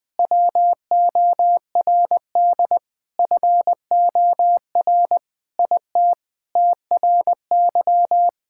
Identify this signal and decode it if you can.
Morse code, heard as a keyed tone, about 21 wpm, 700 Hz. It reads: WORD FOR IT TRY
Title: Sentences with 7-Word Phrases - Spoken Beginning - 20wpm